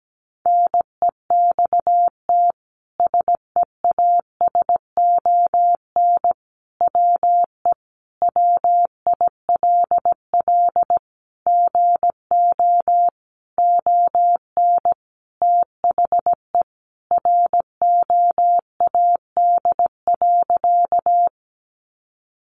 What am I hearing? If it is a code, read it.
Morse code, heard as a keyed tone, about 17 wpm, 700 Hz. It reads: NEXT SEASON WE WILL GO ON THE ROAD.
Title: Sentences from Top 1400 Words 17wpm